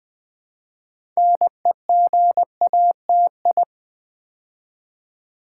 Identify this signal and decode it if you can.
Morse code, heard as a keyed tone, about 20 wpm, 700 Hz. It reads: NEGATI
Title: Build A Word - 2 Letters At A Time - 8-Letter Words - 20wpm